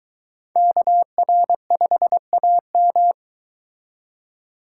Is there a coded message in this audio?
Yes: Morse code from 0.6 to 3.1 s, heard as a keyed tone, about 23 wpm, 700 Hz. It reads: KR5AM